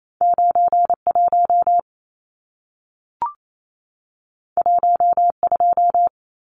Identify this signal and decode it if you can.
Morse code, heard as a keyed tone, about 28 wpm, 700 Hz. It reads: 91 12
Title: Numbers - 2-Digit 28wpm